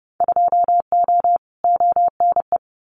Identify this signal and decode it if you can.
Morse code, heard as a keyed tone, about 30 wpm, 700 Hz. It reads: 2O ONE
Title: Winter Field Day Exchanges Encore 30wpm